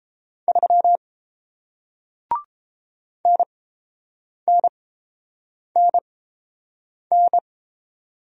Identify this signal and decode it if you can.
Morse code, heard as a keyed tone, about 29 wpm, spaced as for 7 wpm, 700 Hz. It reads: 3 NNNN